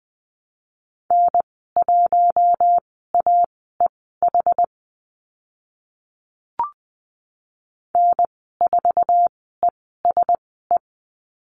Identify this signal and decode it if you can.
Morse code, heard as a keyed tone, about 20 wpm, spaced as for 15 wpm, 700 Hz. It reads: N1AEH N4ESE
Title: 069 Lesson 12g - Letter d - Call Signs 20wpm